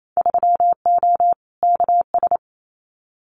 Morse code, heard as a keyed tone, about 28 wpm, 690 Hz. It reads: 3O KS
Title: Winter Field Day Exchanges Encore 28wpm